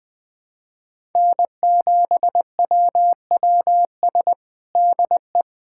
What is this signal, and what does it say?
Morse code, heard as a keyed tone, about 21 wpm, 700 Hz. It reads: N7WWS DE
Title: USA Callsign DE Callsign - 5 Characters - 20wpm